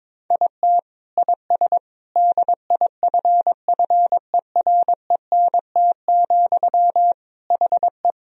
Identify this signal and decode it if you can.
Morse code, heard as a keyed tone, about 22 wpm, 700 Hz. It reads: IT IS DIFFERENT, HE
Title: Sentences from Top 1000 Words - Review Part 3 - 22wpm